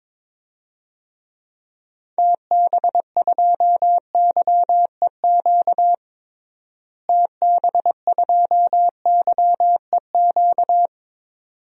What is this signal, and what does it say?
Morse code, heard as a keyed tone, about 22 wpm, 700 Hz. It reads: TB2YEQ TB2YEQ
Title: DXCC-100 Call Signs - 6 Characters - Repeated 2x - 22wpm